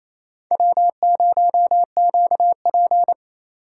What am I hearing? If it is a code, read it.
Morse code, heard as a keyed tone, about 28 wpm, 700 Hz. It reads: W0QP
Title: USA Call Signs - 4 Characters - 4x Character Spacing - 28wpm